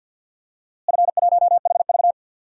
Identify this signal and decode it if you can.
Morse code, heard as a keyed tone, about 47 wpm, 700 Hz. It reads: F1HV
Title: DXCC-100 Call Signs - 4 Characters - Repeated 3x - 50wpm